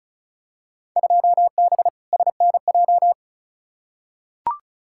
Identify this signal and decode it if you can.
Morse code, heard as a keyed tone, about 34 wpm, 700 Hz. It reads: 2B SNJ
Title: Field Day Exchanges Encore - 35wpm